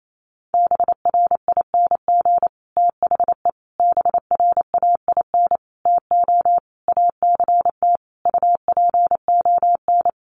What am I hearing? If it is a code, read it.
Morse code, heard as a keyed tone, about 28 wpm, 700 Hz. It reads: BRING THE BRAIN TO ACT UPON
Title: Sentences from Top 1200 Words 28wpm